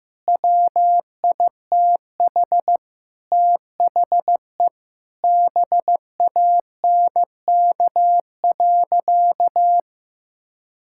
Morse code, heard as a keyed tone, about 15 wpm, 700 Hz. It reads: WITH THE BANK.
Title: Sentences from Top 1000 Words - Review Part 6 - 15wpm